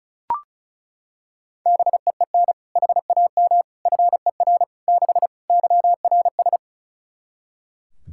Morse code, heard as a keyed tone, about 36 wpm, 700 Hz. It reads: BEEN HAM FER 6 YRS
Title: Practice QSO - Exchange 3 - More Info - 35wpm